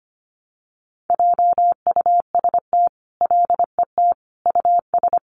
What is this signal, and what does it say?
Morse code, heard as a keyed tone, about 25 wpm, 700 Hz. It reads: JUST LET US
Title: Sentences from Top 500 Words 25wpm